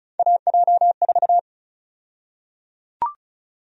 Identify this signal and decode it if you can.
Morse code, heard as a keyed tone, about 34 wpm, 700 Hz. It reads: AJ4
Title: USA Call Sign Prefixes - 35wpm